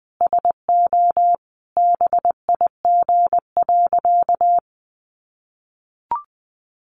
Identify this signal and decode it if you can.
Morse code, heard as a keyed tone, about 20 wpm, 700 Hz. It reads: SO BIG.
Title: Sentences from Top 600 Words 20wpm